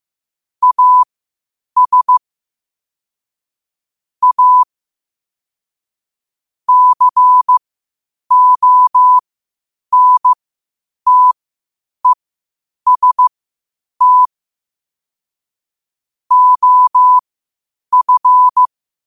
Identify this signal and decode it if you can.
Morse code, heard as a keyed tone, about 15 wpm, spaced as for 8 wpm, 1000 Hz. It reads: AS A CONTEST OF
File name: qotdfiles / 8wpm